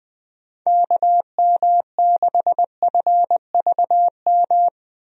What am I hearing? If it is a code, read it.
Morse code, heard as a keyed tone, about 21 wpm, 700 Hz. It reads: KM6FVM
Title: USA Call Signs - 6 Characters - Repeated 2x - 20wpm